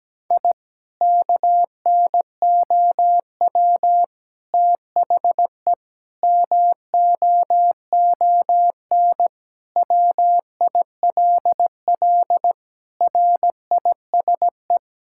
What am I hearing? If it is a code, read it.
Morse code, heard as a keyed tone, about 17 wpm, 700 Hz. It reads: I KNOW THE MOON WILL RISE
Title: Sentences from Top 1000 Words - Review Part 6 - 17wpm